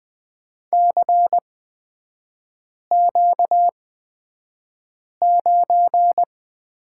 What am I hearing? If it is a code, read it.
Morse code, heard as a keyed tone, about 20 wpm, spaced as for 5 wpm, 700 Hz. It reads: CQ9
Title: 236 Lesson 38e - Letter q - Three Characters 20wpm